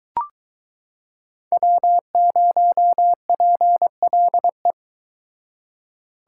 Morse code, heard as a keyed tone, about 24 wpm, 700 Hz. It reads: W0PLE